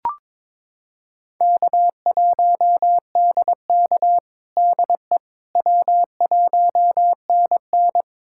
Morse code, heard as a keyed tone, about 22 wpm, 700 Hz. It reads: K1DK DE W1NN
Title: USA Callsign DE Callsign - 4 Characters - 22wpm